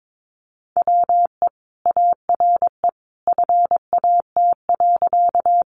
Morse code, heard as a keyed tone, about 22 wpm, 700 Hz. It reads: WE ARE FAT.
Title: Sentences from Top 600 Words 22wpm